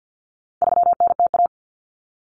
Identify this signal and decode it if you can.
Morse code, heard as a keyed tone, about 50 wpm, 700 Hz. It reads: 3NTA
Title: Common Cut Numbers - 4-Digit 50wpm